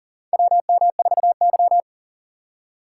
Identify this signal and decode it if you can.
Morse code, heard as a keyed tone, about 40 wpm, 700 Hz. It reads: WM4Y
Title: USA Call Signs - 4 Characters - 3x Character Spacing - 40wpm